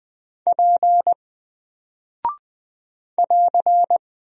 Morse code, heard as a keyed tone, about 20 wpm, 700 Hz. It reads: P <AR>
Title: Single Character (Letter:Number:Punctuation:Prosign) 20wpm